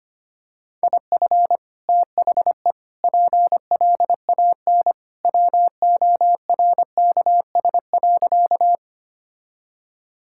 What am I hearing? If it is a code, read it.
Morse code, heard as a keyed tone, about 25 wpm, 690 Hz. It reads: IF THE PLAN WORKS.
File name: practice / 25wpm / Sentences with 2-Word Phrases - 25wpm